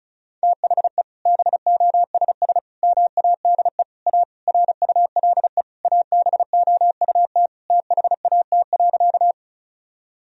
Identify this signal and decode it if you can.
Morse code, heard as a keyed tone, about 34 wpm, 700 Hz. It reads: THE BOSS MADE A RULE ABOUT THAT.
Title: Sentences from Top 1000 Words - Review Part 4 - 35wpm